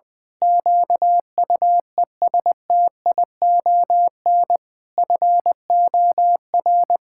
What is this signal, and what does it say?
Morse code, heard as a keyed tone, about 20 wpm, 700 Hz. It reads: QUESTION FOR
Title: Pangram Challenge v2 - 20wpm